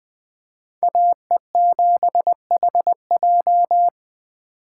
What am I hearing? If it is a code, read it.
Morse code, heard as a keyed tone, about 24 wpm, 700 Hz. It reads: AE7HJ
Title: Speed Racing - USA 2x2 Call Signs 20wpm